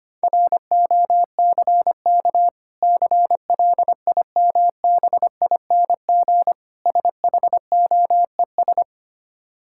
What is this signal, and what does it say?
Morse code, heard as a keyed tone, about 25 wpm, 690 Hz. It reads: ROCK CLIMBING SHOES
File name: practice / 25wpm / Difficult Words with Hints - 25wpm